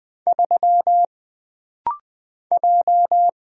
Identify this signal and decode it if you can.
Morse code, heard as a keyed tone, about 20 wpm, 700 Hz. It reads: 3 J
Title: Single Letter-Number Alt - Rapid-Fire 20wpm